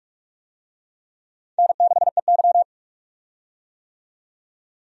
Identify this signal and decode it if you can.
Morse code, heard as a keyed tone, about 43 wpm, 700 Hz. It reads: N6EY
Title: USA Call Signs - 4 Characters - 45wpm